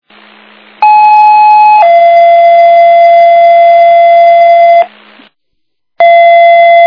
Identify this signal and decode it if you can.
Two-tone page, heard as a one-second tone, then a three-second tone, about 850 then 680 Hz. It is heard from 0.8 to 4.8 s.